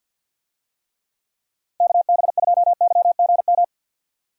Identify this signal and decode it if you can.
Morse code, heard as a keyed tone, about 46 wpm, 700 Hz. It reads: KB2YGM
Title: USA Call Signs - 6 Characters - 50wpm